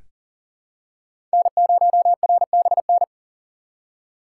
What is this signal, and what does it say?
Morse code, heard as a keyed tone, about 40 wpm, 700 Hz. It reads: N0RBN
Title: USA Call Signs - 5 Characters - 2x Character Spacing - 40wpm